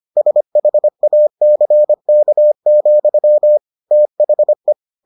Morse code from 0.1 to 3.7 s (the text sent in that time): SHACK,